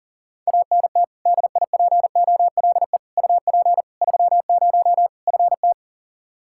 Morse code, heard as a keyed tone, about 40 wpm, 700 Hz. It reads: ANT DIPOLE UP 30 FT
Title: K1USN SST QSO Practice 40wpm